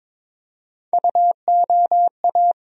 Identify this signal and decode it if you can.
Morse code, heard as a keyed tone, about 22 wpm, 700 Hz. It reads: UOA
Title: USA Call Sign Postfixes - 3 Letters 22wpm